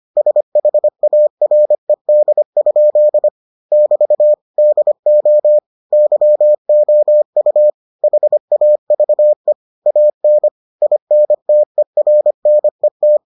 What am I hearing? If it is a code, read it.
Morse code, heard as a keyed tone, about 25 wpm, 600 Hz. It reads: SHARED? <BT> DO YOU HAVE AN INTERNET